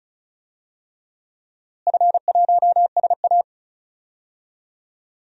Morse code, heard as a keyed tone, about 36 wpm, 700 Hz. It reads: F1SA
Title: DXCC-100 Call Signs - 4 Characters - Repeated 3x - 35wpm